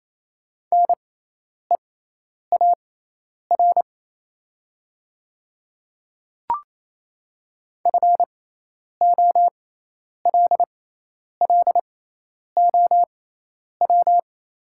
Morse code, heard as a keyed tone, about 28 wpm, spaced as for 10 wpm, 700 Hz. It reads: NEAR FOLLOW
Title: 6x Character Spacing - Top 200 Words - 28wpm